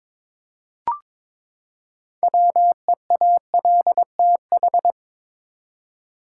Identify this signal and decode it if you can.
Morse code, heard as a keyed tone, about 22 wpm, 700 Hz. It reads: WEALTH
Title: Six Letter Words - 22wpm